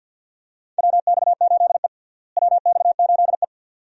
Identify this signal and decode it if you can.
Morse code, heard as a keyed tone, about 53 wpm, 700 Hz. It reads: WX8E WX8E
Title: USA Call Signs - 4 Characters - Repeated 2x - 50wpm